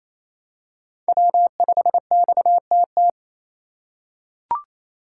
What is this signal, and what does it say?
Morse code, heard as a keyed tone, about 28 wpm, 700 Hz. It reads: W5XTT
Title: USA Call Signs - 3x Character Spacing - 28wpm